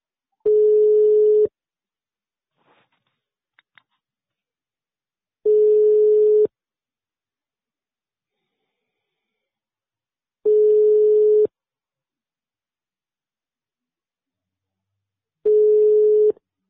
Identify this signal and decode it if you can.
Ringback tone (Europe, 425 Hz).